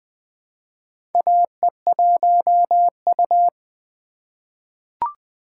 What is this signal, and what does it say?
Morse code, heard as a keyed tone, about 21 wpm, 700 Hz. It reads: AE1U
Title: USA Call Signs - 4 Characters - 8x Character Spacing - 20wpm